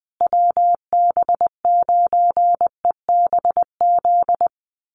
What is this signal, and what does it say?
Morse code, heard as a keyed tone, about 20 wpm, 700 Hz. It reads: WB9EBZ
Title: USA Call Signs Repeated 3x 20wpm